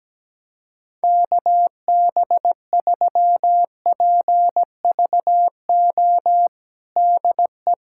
Morse code, heard as a keyed tone, about 17 wpm, 700 Hz. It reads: KB3PVO DE
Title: USA Callsign DE Callsign - 6 Characters - 17wpm